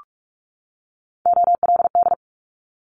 Morse code, heard as a keyed tone, about 44 wpm, 700 Hz. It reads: OLD
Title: Top 500 Words 45wpm